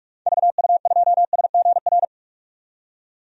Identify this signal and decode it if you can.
Morse code, heard as a keyed tone, about 46 wpm, 700 Hz. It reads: VU2SGR
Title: DXCC-100 Call Signs - 6 Characters - 45wpm